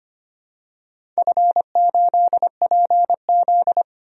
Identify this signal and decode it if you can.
Morse code, heard as a keyed tone, about 25 wpm, 700 Hz. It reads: F8PZ
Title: DXCC-100 Call Signs - 4 Characters - 25wpm